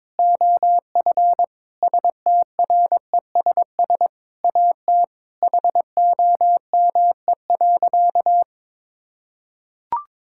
Morse code, heard as a keyed tone, about 22 wpm, 700 Hz. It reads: OF STRESS AT HOME.
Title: Sentences from Top 1000 Words - Review Part 6 - 22wpm